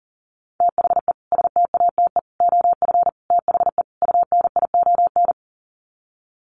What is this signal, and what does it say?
Morse code, heard as a keyed tone, about 40 wpm, 700 Hz. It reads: THE STATE OF THE UNION
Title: Sets of 5 Words Encore 40wpm